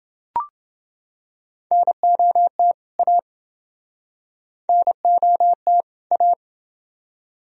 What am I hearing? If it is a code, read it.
Morse code, heard as a keyed tone, about 29 wpm, 700 Hz. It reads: NOT A NOT A